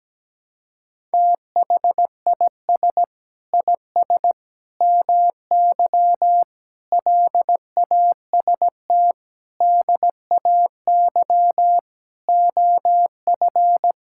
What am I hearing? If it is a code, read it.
Morse code, heard as a keyed tone, about 17 wpm, 700 Hz. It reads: THIS IS MY LAST DAY OF
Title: Sentences from Top 1000 Words - Review Part 5 - 17wpm